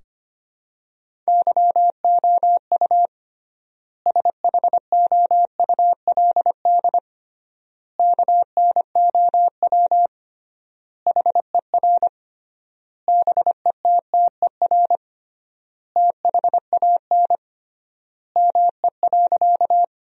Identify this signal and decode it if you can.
Morse code, heard as a keyed tone, about 25 wpm, 700 Hz. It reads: YOU SHOULD KNOW HER BETTER THAN ME.
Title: 3x Word-Spacing - Sentences from Top 400 Words - 25wpm